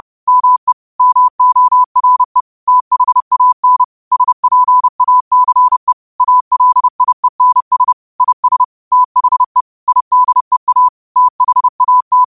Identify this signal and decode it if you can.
Morse code, heard as a keyed tone, about 30 wpm, 1000 Hz. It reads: ME MORE THAN SPACE ALIENS IS THE IDEA THAT
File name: qotdfiles / 30wpm